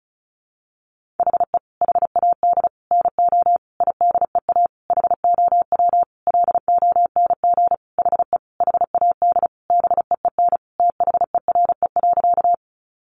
Morse code, heard as a keyed tone, about 34 wpm, 700 Hz. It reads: HE HAD NO IDEA HOW LONG HE HAD BEEN THERE.
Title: Sentences from Top 300 Words 35wpm